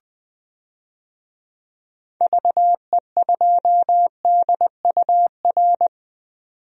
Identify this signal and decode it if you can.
Morse code, heard as a keyed tone, about 20 wpm, 700 Hz. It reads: VE2DUR
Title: DXCC-100 Call Signs - 6 Characters - 20wpm